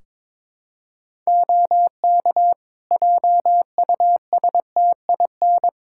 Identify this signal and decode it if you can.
Morse code, heard as a keyed tone, about 22 wpm, 700 Hz. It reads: OK JUSTIN
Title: Practice QSO - Exchange 3 - More Info - 22wpm